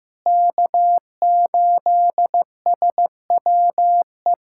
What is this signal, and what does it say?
Morse code, heard as a keyed tone, about 15 wpm, 700 Hz. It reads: K8SWE
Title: USA Callsign DE Callsign 15wpm